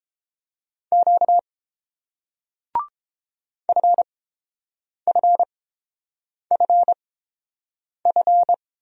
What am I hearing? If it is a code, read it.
Morse code, heard as a keyed tone, about 31 wpm, spaced as for 7 wpm, 700 Hz. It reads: Q FFFF